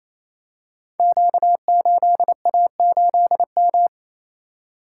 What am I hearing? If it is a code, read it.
Morse code, heard as a keyed tone, about 28 wpm, 700 Hz. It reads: Q8A8M
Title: Code Groups - 5 Letter:Numbers - 28wpm